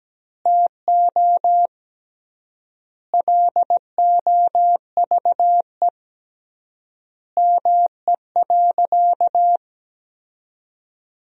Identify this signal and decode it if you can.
Morse code, heard as a keyed tone, about 17 wpm, 700 Hz. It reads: TO LOVE ME.
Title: 3x Word-Spacing - Sentences from Top 400 Words - 17wpm